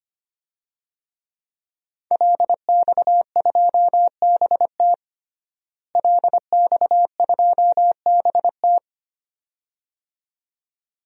Morse code, heard as a keyed tone, about 25 wpm, 700 Hz. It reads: LX2BT LX2BT
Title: DXCC-100 Call Signs - 5 Characters - Repeated 2x - 25wpm